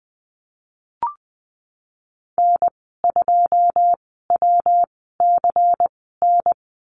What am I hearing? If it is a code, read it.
Morse code, heard as a keyed tone, about 20 wpm, spaced as for 15 wpm, 700 Hz. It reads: N2WCN
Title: 167 Lesson 27g - Number 7 - Call Signs 20wpm